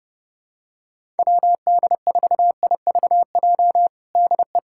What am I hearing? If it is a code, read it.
Morse code, heard as a keyed tone, about 30 wpm, 700 Hz. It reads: WD4IVJ DE
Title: USA Callsign DE Callsign 30wpm